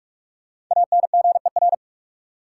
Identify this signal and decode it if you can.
Morse code, heard as a keyed tone, about 43 wpm, 700 Hz. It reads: ANGER
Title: Five Letter Words - 45wpm